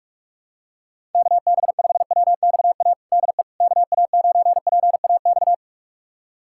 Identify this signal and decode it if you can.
Morse code, heard as a keyed tone, about 46 wpm, 700 Hz. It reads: KB5WXA DE KA9PAX
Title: USA Callsign DE Callsign - 6 Characters - 45wpm